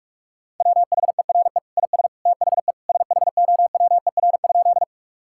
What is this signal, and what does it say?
Morse code, heard as a keyed tone, about 46 wpm, 700 Hz. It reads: WHERE IS THE SHOWER?
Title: Sentences from Top 1600 Words 45wpm